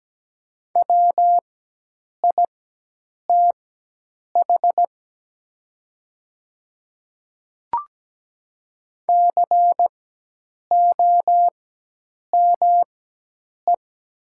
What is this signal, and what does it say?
Morse code, heard as a keyed tone, about 17 wpm, spaced as for 8 wpm, 700 Hz. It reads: WITH COME
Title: 4x Character Spacing - Top 100 Words - 17wpm